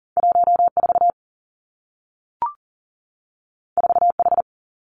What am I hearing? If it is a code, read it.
Morse code, heard as a keyed tone, about 41 wpm, 700 Hz. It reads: 14 4H
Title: Two Challenging Letter-Number Combinations 40wpm